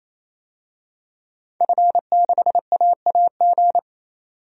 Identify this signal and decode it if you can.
Morse code, heard as a keyed tone, about 28 wpm, 700 Hz. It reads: F6AAG
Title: DXCC-100 Call Signs - 5 Characters - 28wpm